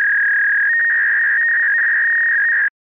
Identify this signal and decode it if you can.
Fax machine handshake.